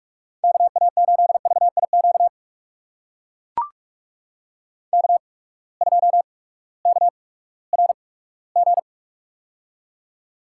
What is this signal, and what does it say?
Morse code, heard as a keyed tone, about 44 wpm, spaced as for 20 wpm, 690 Hz. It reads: KA8VIQ K2KRG
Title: USA Call Signs - 8x Character Spacing - 45wpm